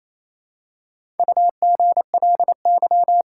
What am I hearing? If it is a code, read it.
Morse code, heard as a keyed tone, about 28 wpm, 700 Hz. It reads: UGLY
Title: Four Letter Words - 28wpm